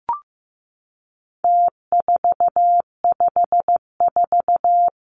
Morse code, heard as a keyed tone, about 15 wpm, 700 Hz. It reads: T454
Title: Numbers - Contest Style - 4-Digit 15wpm